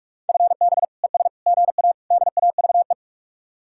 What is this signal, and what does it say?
Morse code, heard as a keyed tone, about 44 wpm, 700 Hz. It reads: FB ES GA DAVE